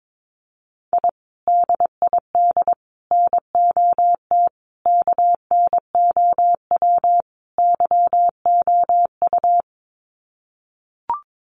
Morse code, heard as a keyed tone, about 22 wpm, 700 Hz. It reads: I DID NOT KNOW YOU
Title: Sets of 5 Words Encore2 22wpm